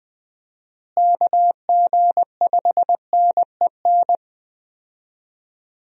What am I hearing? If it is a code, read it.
Morse code, heard as a keyed tone, about 20 wpm, 700 Hz. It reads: KG5NEN
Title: USA Call Signs - 6 Characters - Repeated 2x - 20wpm